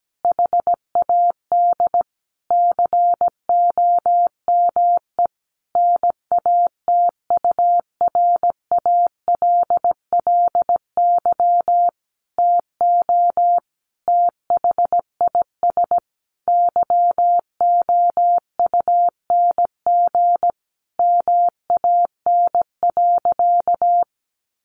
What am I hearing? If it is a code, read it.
Morse code, heard as a keyed tone, about 17 wpm, 700 Hz. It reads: HAD COME NATURALLY TO THIS YOUNG MAN.
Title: Sentences from Top 1300 Words 17wpm